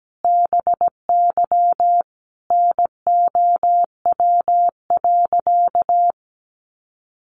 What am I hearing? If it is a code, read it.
Morse code, heard as a keyed tone, about 17 wpm, 700 Hz. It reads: BY NOW.
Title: Sentences from Top 1400 Words 17wpm